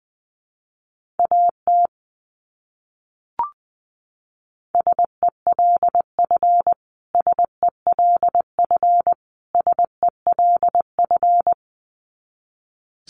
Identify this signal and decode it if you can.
Morse code, heard as a keyed tone, about 20 wpm, 700 Hz. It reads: AT SELF SELF SELF
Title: Top 200 Words Repeated 20wpm